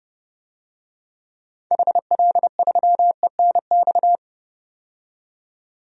Morse code, heard as a keyed tone, about 30 wpm, 700 Hz. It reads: HL3ENX
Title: DXCC-100 Call Signs - 6 Characters - 30wpm